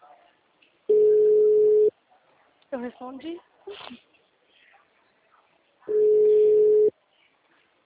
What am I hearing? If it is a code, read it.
Ringback tone (Europe, 425 Hz), with speech.
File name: out- -2032- - - .91986